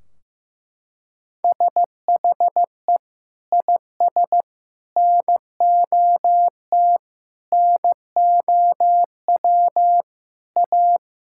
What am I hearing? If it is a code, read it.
Morse code, heard as a keyed tone, about 15 wpm, 700 Hz. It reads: SHE IS NOT NOW A
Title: Sentences from Top 1600 Words 15wpm